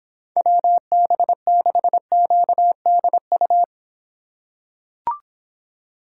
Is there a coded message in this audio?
Yes: Morse code from 0.4 to 3.6 s, heard as a keyed tone, about 26 wpm, 700 Hz. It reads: WB6QDU